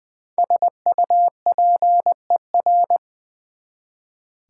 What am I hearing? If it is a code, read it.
Morse code, heard as a keyed tone, about 21 wpm, 700 Hz. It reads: SUPER
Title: Prefixes and Suffixes 20wpm